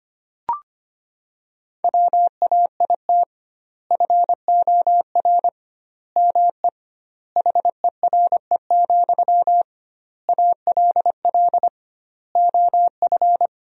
Morse code, heard as a keyed tone, about 25 wpm, 700 Hz. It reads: WAIT FOR ME HERE, ALL OF
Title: 2x Word-Spacing - Sentences from Top 500 Words - 25wpm